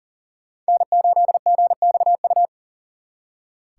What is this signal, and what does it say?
Morse code, heard as a keyed tone, about 52 wpm, 700 Hz. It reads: N8GXU